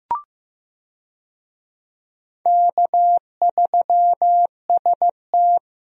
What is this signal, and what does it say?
Morse code, heard as a keyed tone, about 15 wpm, 700 Hz. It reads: K3ST
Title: USA Call Signs - 4 Characters - Repeated 2x - 15wpm